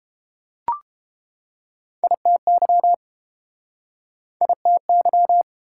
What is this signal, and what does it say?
Morse code, heard as a keyed tone, about 30 wpm, 700 Hz. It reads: ITY ITY